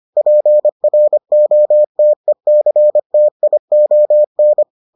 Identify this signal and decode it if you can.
Morse code, heard as a keyed tone, about 25 wpm, 600 Hz. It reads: PROTECTION